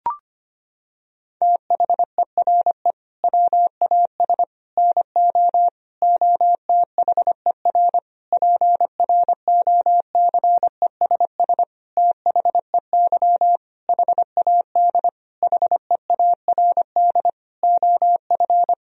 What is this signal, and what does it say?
Morse code, heard as a keyed tone, about 25 wpm, 700 Hz. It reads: THERE WAS NO OTHER PROCESS THEY HAD HEARD OF
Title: Sentences from Top 1000 Words - Review Part 3 - 25wpm